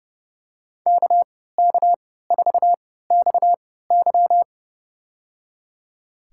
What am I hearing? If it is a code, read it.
Morse code, heard as a keyed tone, about 30 wpm, spaced as for 17 wpm, 700 Hz. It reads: KK4XY
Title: USA Call Signs - 3x Character Spacing - 30wpm